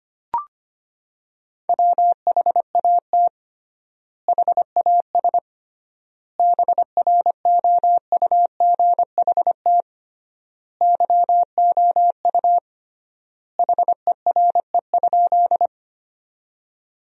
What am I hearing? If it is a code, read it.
Morse code, heard as a keyed tone, about 25 wpm, 700 Hz. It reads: WHAT HAS BROUGHT YOU HERE?
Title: 3x Word-Spacing - Sentences from Top 500 Words - 25wpm